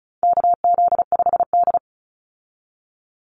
Morse code, heard as a keyed tone, about 34 wpm, 700 Hz. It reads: KZ5D
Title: Call Signs Repeated 2x - 35wpm